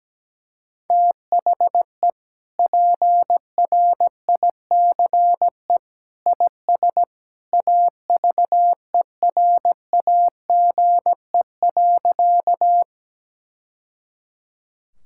Morse code, heard as a keyed tone, about 17 wpm, 700 Hz. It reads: THE PRICE IS AVERAGE.
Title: Sentences from Top 1000 Words - Review Part 6 - 17wpm